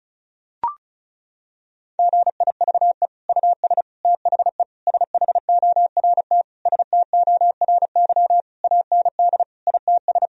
Morse code, heard as a keyed tone, about 34 wpm, 700 Hz. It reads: GIVE US THE SHORT STORY AND ITS
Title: Sentences from Top 1000 Words - Review Part 4 - 35wpm